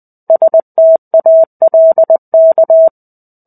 Morse code, heard as a keyed tone, about 20 wpm, 650 Hz. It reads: STALK
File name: 05-Chapter-05 20 WPM